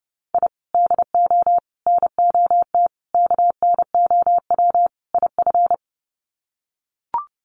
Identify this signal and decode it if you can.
Morse code, heard as a keyed tone, about 30 wpm, 700 Hz. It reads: I DO NOT KNOW IF
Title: Sets of 5 Words 30wpm